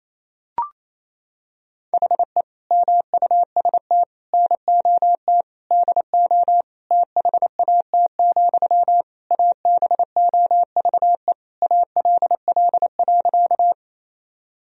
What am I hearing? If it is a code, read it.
Morse code, heard as a keyed tone, about 28 wpm, 700 Hz. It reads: HE MUST NOT DO THAT, ABOVE ALL.
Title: Sentences from Top 1000 Words - Review Part 2 - 28wpm